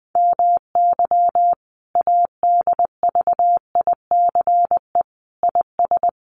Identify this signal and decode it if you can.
Morse code, heard as a keyed tone, about 20 wpm, 700 Hz. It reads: MY ADVICE IS
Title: Sentences from Top 1300 Words 20wpm